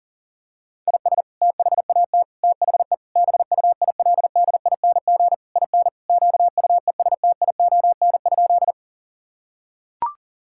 Morse code, heard as a keyed tone, about 40 wpm, 700 Hz. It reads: IS THAT THE BUILDING IN QUESTION?
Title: Sentences from Top 1000 Words - Review Part 5 - 40wpm